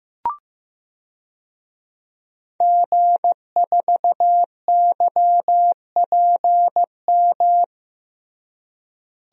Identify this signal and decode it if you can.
Morse code, heard as a keyed tone, about 16 wpm, 700 Hz. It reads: G4YPM
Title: DXCC-100 Call Signs - 5 Characters - Repeated 2x - 15wpm